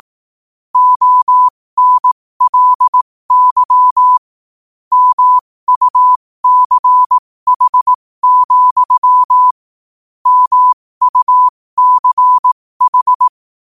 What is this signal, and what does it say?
Morse code, heard as a keyed tone, about 18 wpm, spaced as for 15 wpm, 1000 Hz. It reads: ONLY MUCH, MUCH
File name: qotdfiles / 15wpm